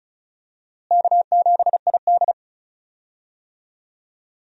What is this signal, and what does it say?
Morse code, heard as a keyed tone, about 34 wpm, 700 Hz. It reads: K7ID
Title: USA Call Signs - 4 Characters - Repeated 2x - 35wpm